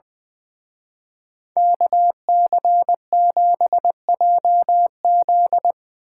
Morse code, heard as a keyed tone, about 22 wpm, 700 Hz. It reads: KC7JZ
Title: Speed Racing - USA 2x2 Call Signs 20wpm